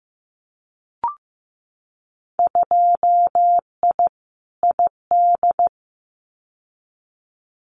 Morse code, heard as a keyed tone, about 15 wpm, 700 Hz. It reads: 2I ID
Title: Winter Field Day Exchanges Encore 15wpm